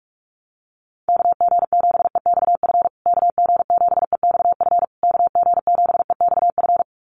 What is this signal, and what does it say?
Morse code, heard as a keyed tone, about 44 wpm, 700 Hz. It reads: KG7EXF KG7EXF KG7EXF
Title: USA Call Signs Repeated 3x 45wpm